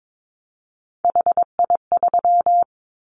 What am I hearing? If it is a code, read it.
Morse code, heard as a keyed tone, about 22 wpm, 700 Hz. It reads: HI3
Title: Callsign Prefixes for Top 100 DXCC - 22wpm